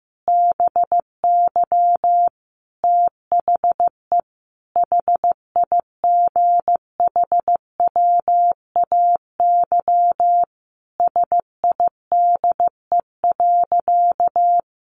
Morse code, heard as a keyed tone, about 15 wpm, 700 Hz. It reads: BY THE HIGHWAY SIDE.
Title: Sentences from Top 1400 Words 15wpm